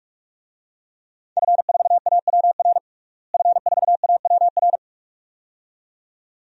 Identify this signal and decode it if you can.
Morse code, heard as a keyed tone, about 41 wpm, 700 Hz. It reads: F4AWR F4AWR
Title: DXCC-100 Call Signs - 5 Characters - Repeated 2x - 45wpm